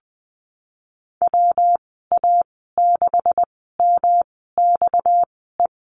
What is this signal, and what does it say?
Morse code, heard as a keyed tone, about 20 wpm, spaced as for 15 wpm, 700 Hz. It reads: WA6MXE
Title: 238 Lesson 38g - Letter q - Call Signs 20wpm